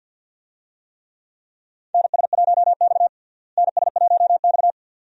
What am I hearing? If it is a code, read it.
Morse code, heard as a keyed tone, about 49 wpm, 700 Hz. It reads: NS1X NS1X
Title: USA Call Signs - 4 Characters - Repeated 2x - 50wpm